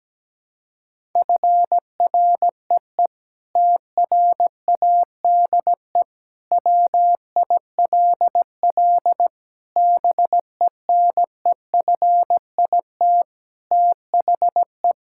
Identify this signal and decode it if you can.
Morse code, heard as a keyed tone, about 17 wpm, 700 Hz. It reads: FREE TRADE WILL BENEFIT THE
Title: Sentences from Top 1000 Words - Review Part 6 - 17wpm